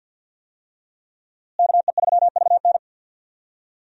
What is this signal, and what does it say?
Morse code, heard as a keyed tone, about 46 wpm, 700 Hz. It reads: KE3VN
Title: USA Call Signs - 5 Characters - Repeated 3x - 50wpm